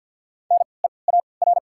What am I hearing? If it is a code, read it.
Morse code, heard as a keyed tone, about 53 wpm, 700 Hz. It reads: N E A R
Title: 3x Character Spacing - Top 200 Words - 50wpm